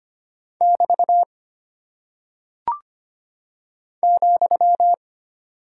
Morse code, heard as a keyed tone, about 25 wpm, 700 Hz. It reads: <BT> ,
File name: Single Punctuation-Prosign - 25wpm